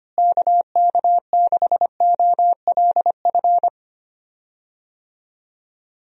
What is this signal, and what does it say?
Morse code, heard as a keyed tone, about 25 wpm, 700 Hz. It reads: KK6OLF
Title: USA Call Signs - 6 Characters - Repeated 3x - 25wpm